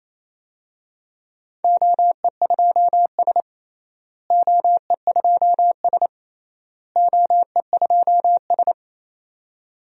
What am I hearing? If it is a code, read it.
Morse code, heard as a keyed tone, about 28 wpm, 700 Hz. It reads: OE2S OE2S OE2S
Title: DXCC-100 Call Signs - 4 Characters - Repeated 3x - 28wpm